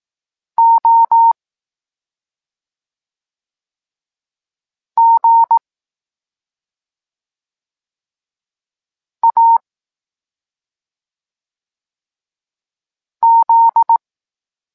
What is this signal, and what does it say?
Morse code, heard as a keyed tone, about 18 wpm, 920 Hz. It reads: O G A Z
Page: Tekst 23.3 – Ciąg dalszy doskonalenia odbioru poznanych liter